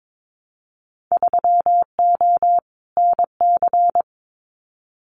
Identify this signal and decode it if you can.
Morse code, heard as a keyed tone, about 22 wpm, 700 Hz. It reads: 3O NC
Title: Winter Field Day Exchanges 22wpm